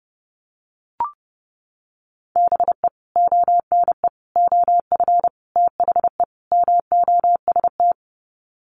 Morse code, heard as a keyed tone, about 30 wpm, 700 Hz. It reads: BE ONE OF THE MOST
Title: Sets of 5 Words Encore 30wpm